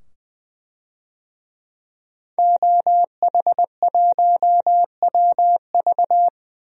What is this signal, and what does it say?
Morse code, heard as a keyed tone, about 20 wpm, 700 Hz. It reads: OH1WV
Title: DXCC-100 Call Signs - 5 Characters - 20wpm